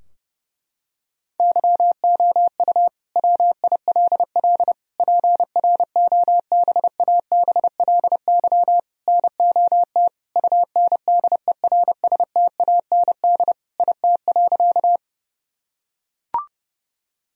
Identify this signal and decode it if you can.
Morse code, heard as a keyed tone, about 32 wpm, 700 Hz. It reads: YOU WILL PROBABLY NOT UNDERSTAND IT.
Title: Sentences from Top 1000 Words - Review Part 4 - 30wpm